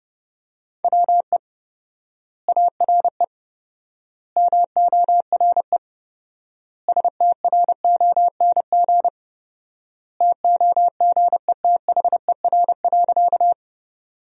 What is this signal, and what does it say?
Morse code, heard as a keyed tone, about 30 wpm, 700 Hz. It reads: WE ARE MORE STRONG TOGETHER.
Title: 4x Word-Spacing - Sentences from Top 500 Words - 30wpm